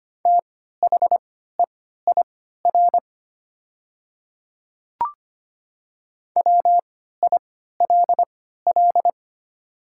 Morse code, heard as a keyed tone, about 25 wpm, spaced as for 14 wpm, 700 Hz. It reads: THEIR WILL
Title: 3x Character Spacing - Top 100 Words - 25wpm